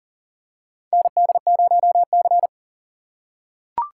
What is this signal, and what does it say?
Morse code, heard as a keyed tone, about 40 wpm, 700 Hz. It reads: ND0C
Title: USA Call Signs - 4 Characters - 3x Character Spacing - 40wpm